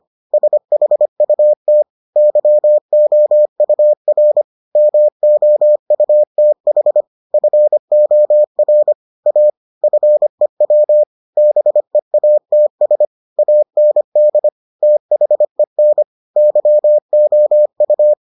Morse code, heard as a keyed tone, about 25 wpm, 600 Hz. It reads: SHUT YOUR MOUTH FOR A FEW BEATS AND THEN YOU